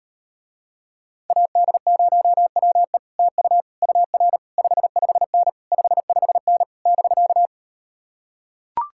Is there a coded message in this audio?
Yes: Morse code from 1.3 to 7.5 s, heard as a keyed tone, about 38 wpm, 700 Hz. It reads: AD0WE TU UR 55N 55N <BK>